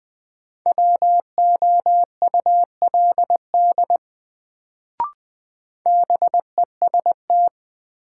Morse code, heard as a keyed tone, about 20 wpm, 700 Hz. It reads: WOULD BEST
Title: Sets of 2 Words - New Word Drill - 20wpm